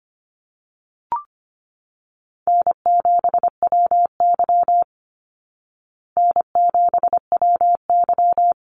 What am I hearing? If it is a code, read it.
Morse code, heard as a keyed tone, about 25 wpm, 700 Hz. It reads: N7WY N7WY
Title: Easy Call Signs Repeated 2x - 25wpm